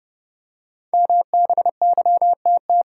Morse code, heard as a keyed tone, about 30 wpm, 700 Hz. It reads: MBYTT
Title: Code Groups - 5 Letter:Numbers - 30wpm